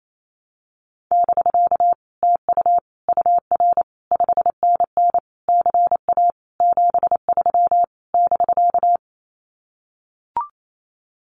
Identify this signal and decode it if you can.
Morse code, heard as a keyed tone, about 28 wpm, 700 Hz. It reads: <BK> TU UR 5NN CA 73 <BK>
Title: Common QSO Phrases Encore 28wpm